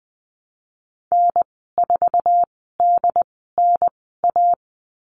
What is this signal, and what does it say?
Morse code, heard as a keyed tone, about 20 wpm, spaced as for 15 wpm, 700 Hz. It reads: N4DNA
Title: 128 Lesson 21g - Number 6 - Call Signs 20wpm